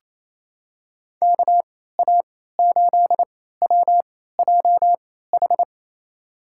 Morse code, heard as a keyed tone, about 27 wpm, spaced as for 16 wpm, 700 Hz. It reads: KA8WJH
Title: USA Call Signs - 6 Characters - 3x Character Spacing - 28wpm